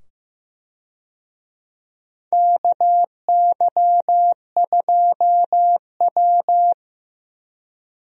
Morse code, heard as a keyed tone, about 15 wpm, 700 Hz. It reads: KY2W
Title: USA Call Signs - 4 Characters - 15wpm